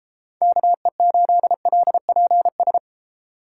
Morse code, heard as a keyed tone, about 32 wpm, 700 Hz. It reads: KE8LPS